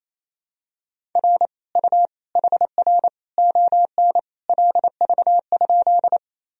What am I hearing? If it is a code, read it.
Morse code, heard as a keyed tone, about 27 wpm, 700 Hz. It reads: R U HR ON LV?
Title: Sentences from Top 400 Words and Abbreviations 28wpm